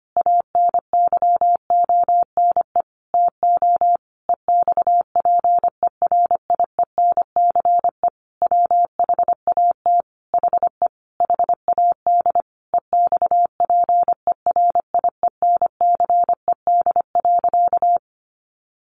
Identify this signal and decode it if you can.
Morse code, heard as a keyed tone, about 25 wpm, 700 Hz. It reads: ANYONE TO EXPERIENCE WHAT HE HAD EXPERIENCED.
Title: Sentences from Top 1100 Words 25wpm